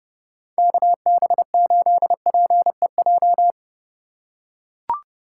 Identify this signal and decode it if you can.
Morse code, heard as a keyed tone, about 29 wpm, 700 Hz. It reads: KB8PEJ